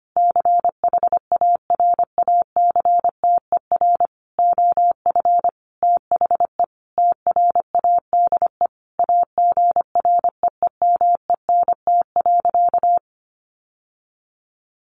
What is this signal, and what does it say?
Morse code, heard as a keyed tone, about 25 wpm, 700 Hz. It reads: CHARACTER OF THE TRADE AGREEMENT.
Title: Sentences from Top 1500 Words 25wpm